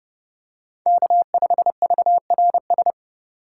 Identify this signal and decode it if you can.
Morse code, heard as a keyed tone, about 31 wpm, 700 Hz. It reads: K5VRS
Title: USA Call Signs - 5 Characters - 3x Character Spacing - 30wpm